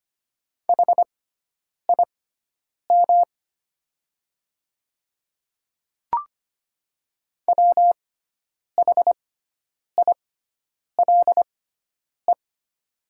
Morse code, heard as a keyed tone, about 25 wpm, spaced as for 9 wpm, 700 Hz. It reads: HIM WHILE
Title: 6x Character Spacing - Top 300 Words - 25wpm